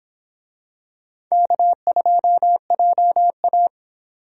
Morse code, heard as a keyed tone, about 26 wpm, 700 Hz. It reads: K2JA